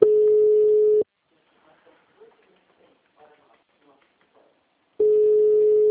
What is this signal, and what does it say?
Ringback tone (Europe, 425 Hz), with speech.